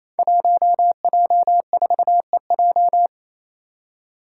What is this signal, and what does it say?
Morse code, heard as a keyed tone, about 28 wpm, 700 Hz. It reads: 1J4EJ
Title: Five Challenging Letter-Number Combinations 28wpm